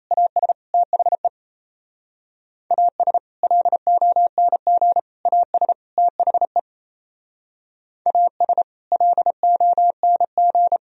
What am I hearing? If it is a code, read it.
Morse code, heard as a keyed tone, about 35 wpm, 700 Hz. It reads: AS THE AS LONG AS THE AS LONG